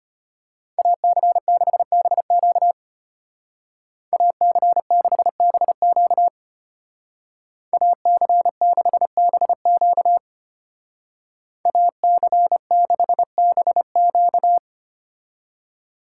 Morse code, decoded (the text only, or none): AC6BQ AC6BQ AC6BQ AC6BQ